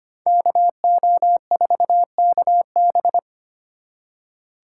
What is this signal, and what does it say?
Morse code, heard as a keyed tone, about 33 wpm, 700 Hz. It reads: KO4KB